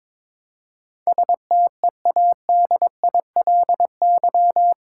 Morse code, heard as a keyed tone, about 22 wpm, 700 Hz. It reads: STEADILY
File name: practice / 22wpm / Eight Letter Words - 22wpm